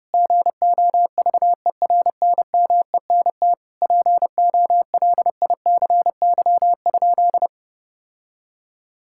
Morse code, heard as a keyed tone, about 32 wpm, 700 Hz. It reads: GOVERNMENT POLICY?
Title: Sentences from Top 1000 Words - Review Part 5 - 30wpm